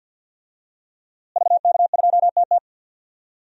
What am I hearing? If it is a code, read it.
Morse code, heard as a keyed tone, about 46 wpm, 700 Hz. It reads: VK2TT
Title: DXCC-100 Call Signs - 5 Characters - Repeated 3x - 50wpm